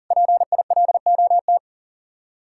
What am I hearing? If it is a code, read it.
Morse code, heard as a keyed tone, about 52 wpm, 700 Hz. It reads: PILOT